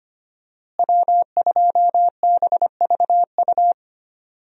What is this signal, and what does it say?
Morse code, heard as a keyed tone, about 25 wpm, 700 Hz. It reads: W2BVU
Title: USA Call Signs - 5 Characters - Repeated 2x - 25wpm